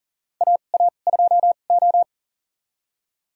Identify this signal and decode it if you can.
Morse code, heard as a keyed tone, about 40 wpm, spaced as for 29 wpm, 700 Hz. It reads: AA2O
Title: USA Call Signs - 4 Characters - 2x Character Spacing - 40wpm